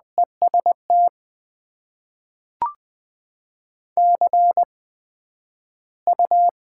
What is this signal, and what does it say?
Morse code, heard as a keyed tone, about 20 wpm, spaced as for 11 wpm, 700 Hz. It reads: EST CU
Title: Top 90 QSO Elements - 8x Character Spacing - 20wpm